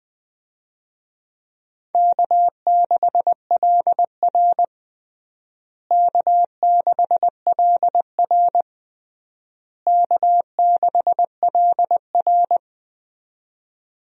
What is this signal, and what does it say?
Morse code, heard as a keyed tone, about 20 wpm, 700 Hz. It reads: K6LR K6LR K6LR
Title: USA Call Signs - 4 Characters - Repeated 3x - 20wpm